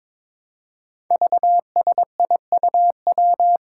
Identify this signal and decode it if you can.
Morse code, heard as a keyed tone, about 22 wpm, 700 Hz. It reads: VSIUW
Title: Five Challenging Letter-Number Combinations 22wpm